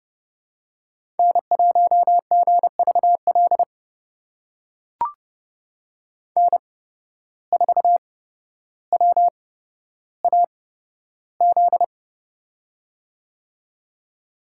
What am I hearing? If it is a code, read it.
Morse code, heard as a keyed tone, about 30 wpm, spaced as for 13 wpm, 700 Hz. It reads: N1GVL N4WAZ
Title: USA Call Signs - 5 Characters - 8x Character Spacing - 30wpm